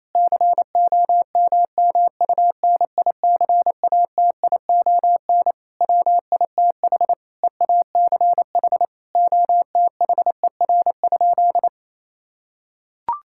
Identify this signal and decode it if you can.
Morse code, heard as a keyed tone, about 28 wpm, 700 Hz. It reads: COMMUNICATION WITH EACH OTHER?
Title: Sentences from Top 1000 Words - Review Part 6 - 28wpm